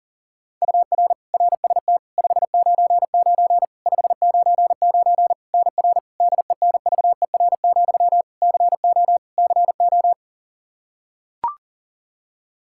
Morse code, heard as a keyed tone, about 40 wpm, 700 Hz. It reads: UR RST 599 599 NR DENVER, CO CO
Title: Practice QSO - Exchange 1 - Beginning - 40wpm